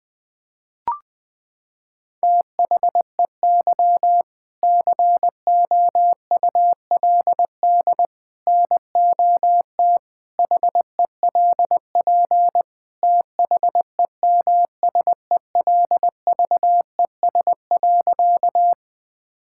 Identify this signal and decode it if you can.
Morse code, heard as a keyed tone, about 20 wpm, 700 Hz. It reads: THEY COULD NOT HELP THEMSELVES.
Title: Sentences from Top 1000 Words - Review Part 5 - 20wpm